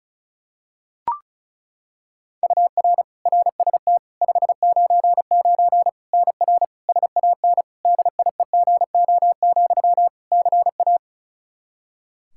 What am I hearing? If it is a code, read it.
Morse code, heard as a keyed tone, about 34 wpm, 700 Hz. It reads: UR RST 599 NR SAN DIEGO, CA
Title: Practice QSO - Exchange 1 - Beginning - 35wpm